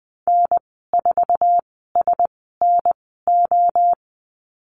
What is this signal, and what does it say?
Morse code, heard as a keyed tone, about 20 wpm, spaced as for 15 wpm, 700 Hz. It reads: N4SNO
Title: 069 Lesson 12g - Letter d - Call Signs 20wpm